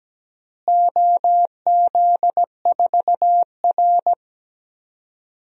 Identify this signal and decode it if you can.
Morse code, heard as a keyed tone, about 17 wpm, 700 Hz. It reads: OZ4R
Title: DXCC-100 Call Signs - 4 Characters - Repeated 2x - 17wpm